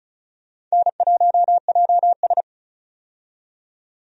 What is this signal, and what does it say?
Morse code, heard as a keyed tone, about 36 wpm, 700 Hz. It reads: N1JS
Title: USA Call Signs - 4 Characters - 35wpm